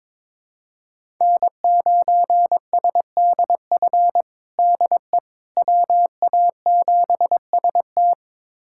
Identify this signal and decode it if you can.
Morse code, heard as a keyed tone, about 22 wpm, 700 Hz. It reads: N9SDF DE WA7ST
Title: USA Callsign DE Callsign - 5 Characters - 22wpm